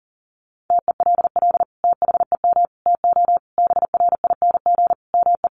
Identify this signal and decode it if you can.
Morse code, heard as a keyed tone, about 40 wpm, 700 Hz. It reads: TELL THEM TO BRING ME
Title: Sentences from Top 1200 Words 40wpm